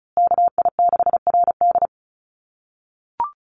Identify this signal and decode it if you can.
Morse code, heard as a keyed tone, about 34 wpm, 700 Hz. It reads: KI6RD
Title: USA Call Signs Repeated 2x 35wpm